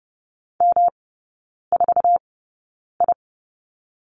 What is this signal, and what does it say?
Morse code, heard as a keyed tone, about 30 wpm, spaced as for 9 wpm, 700 Hz. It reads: M4I
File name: Lesson 18e - Letter m - Three Characters 30wpm